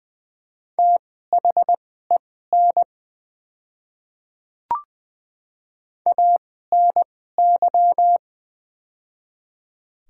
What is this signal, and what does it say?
Morse code, heard as a keyed tone, about 20 wpm, spaced as for 14 wpm, 700 Hz. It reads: THEN ANY
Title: 2x Character Spacing - Top 200 Words - 20wpm